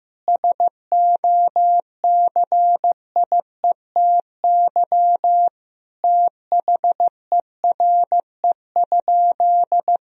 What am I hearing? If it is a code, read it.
Morse code, heard as a keyed tone, about 15 wpm, 700 Hz. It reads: SOCIETY THERE?
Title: Sentences from Top 1000 Words - Review Part 4 - 15wpm